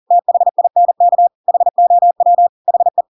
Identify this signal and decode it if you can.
Morse code, heard as a keyed tone, about 40 wpm, 700 Hz. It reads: THINK HOW HE
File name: Pinocchio / Pinocchio ch5 40 40 ik1hge